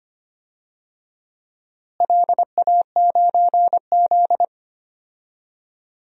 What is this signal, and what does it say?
Morse code, heard as a keyed tone, about 25 wpm, 700 Hz. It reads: LA9Z
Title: DXCC-100 Call Signs - 4 Characters - Repeated 2x - 25wpm